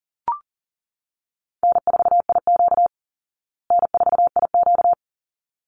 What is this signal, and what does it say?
Morse code, heard as a keyed tone, about 40 wpm, 700 Hz. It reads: N4IQ N4IQ
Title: Call Signs Repeated 3x - 40wpm